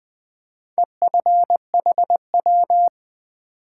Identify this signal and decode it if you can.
Morse code, heard as a keyed tone, about 20 wpm, 700 Hz. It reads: EFHW
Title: Top 180 QSO Elements 20wpm